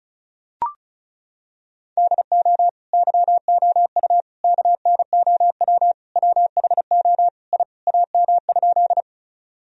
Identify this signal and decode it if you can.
Morse code, heard as a keyed tone, about 36 wpm, 700 Hz. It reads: DO YOU KNOW WHO I AM?
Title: Sentences from Top 1000 Words - Review Part 2 - 35wpm